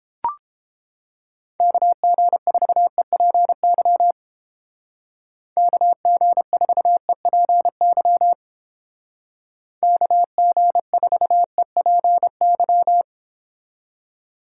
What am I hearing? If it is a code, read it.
Morse code, heard as a keyed tone, about 29 wpm, 700 Hz. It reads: KG4EPY KG4EPY KG4EPY